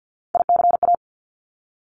Morse code, heard as a keyed tone, about 50 wpm, 700 Hz. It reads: ICA
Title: Top Three Letter Combinations in Words 50wpm